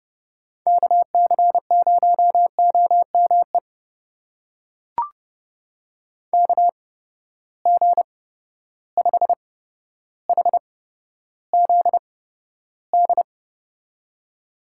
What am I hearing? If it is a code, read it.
Morse code, heard as a keyed tone, about 30 wpm, spaced as for 13 wpm, 700 Hz. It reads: KC0OME KG5HZD
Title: USA Call Signs - 6 Characters - 8x Character Spacing - 30wpm